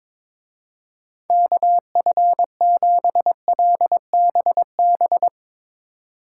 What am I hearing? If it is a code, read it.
Morse code, heard as a keyed tone, about 23 wpm, 700 Hz. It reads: KF7LBB
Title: Speed Racing - USA 2x3 Call Signs 22wpm